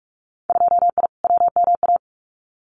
Morse code, heard as a keyed tone, about 46 wpm, 700 Hz. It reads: 2I WMA
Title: Winter Field Day Exchanges Encore 45wpm